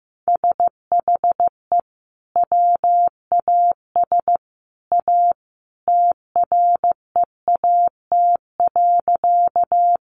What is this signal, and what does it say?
Morse code, heard as a keyed tone, about 15 wpm, 700 Hz. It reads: SHE WAS A TREAT.
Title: Sentences from Top 1300 Words 15wpm